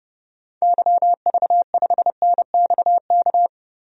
Morse code, heard as a keyed tone, about 30 wpm, 700 Hz. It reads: YV5NXK
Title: DXCC-100 Call Signs - 6 Characters - Repeated 2x - 30wpm